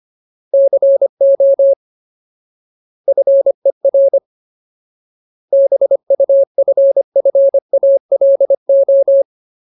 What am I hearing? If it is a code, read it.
Morse code, heard as a keyed tone, about 25 wpm, 550 Hz. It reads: CO FER BUFFALO
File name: Ultimate Word List / 25WPM Ultimate